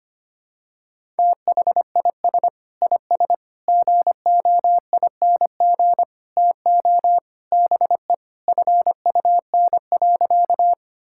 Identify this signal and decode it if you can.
Morse code, heard as a keyed tone, about 25 wpm, 700 Hz. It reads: THIS IS GOING TO BE FUN.
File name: practice / 25wpm / Sentences with 4-Word Phrases - 25wpm